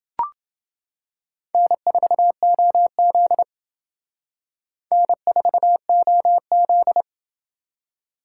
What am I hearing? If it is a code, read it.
Morse code, heard as a keyed tone, about 28 wpm, 700 Hz. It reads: N4OZ N4OZ